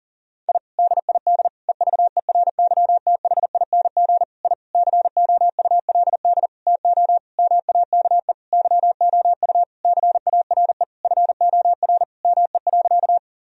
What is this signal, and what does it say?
Morse code, heard as a keyed tone, about 37 wpm, 700 Hz. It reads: I DID EVERYTHING I COULD TO MAKE YOU CARE FOR ME.
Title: Sentences from Top 1000 Words - Review Part 1 - 40wpm